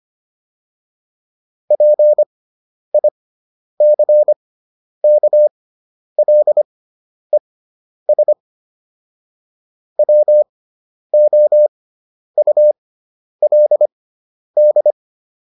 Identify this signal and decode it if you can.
Morse code, heard as a keyed tone, about 25 wpm, spaced as for 10 wpm, 600 Hz. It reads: PICKLES WOULD